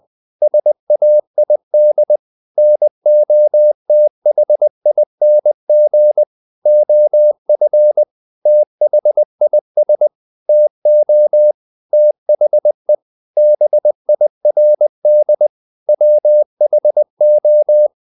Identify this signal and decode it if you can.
Morse code, heard as a keyed tone, about 20 wpm, 600 Hz. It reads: SAID NOTHING OF THIS TO THE BIRD WHO
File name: the old man and the sea / the old man and the sea part 1 20wpm